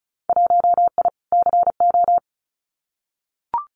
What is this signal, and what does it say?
Morse code, heard as a keyed tone, about 36 wpm, 700 Hz. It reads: 1I CO
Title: Winter Field Day Exchanges Encore 35wpm